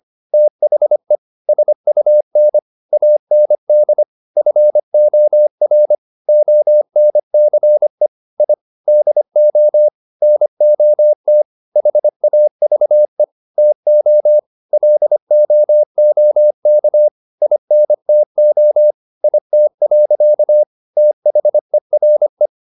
Morse code, heard as a keyed tone, about 25 wpm, 600 Hz. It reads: THE SUN AND FOR ONCE I DO NOT HAVE TO LOOK INTO IT. THERE
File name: the old man and the sea part 1 25wpm